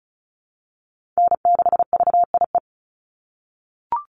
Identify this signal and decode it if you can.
Morse code, heard as a keyed tone, about 34 wpm, 700 Hz. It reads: N6VIE
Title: USA Call Signs 35wpm